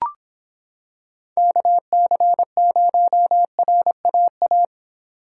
Morse code, heard as a keyed tone, about 25 wpm, 700 Hz. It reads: KC0RAA